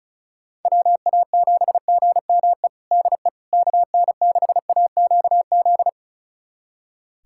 Morse code, heard as a keyed tone, about 36 wpm, 700 Hz. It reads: WA7GME DE KN6AQZ
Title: USA Callsign DE Callsign 35wpm